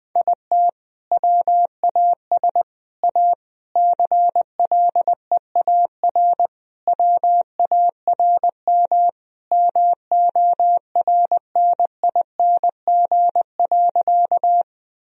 Morse code, heard as a keyed tone, about 20 wpm, 700 Hz. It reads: IT WAS A CLEAR WARM MORNING.
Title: Sentences from Top 1000 Words - Review Part 2 - 20wpm